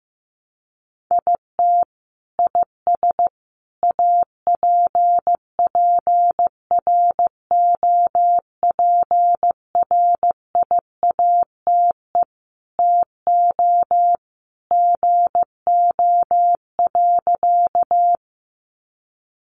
Morse code, heard as a keyed tone, about 14 wpm, 700 Hz. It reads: IT IS APPROPRIATE TO GO.
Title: Sentences from Top 900 Words 15wpm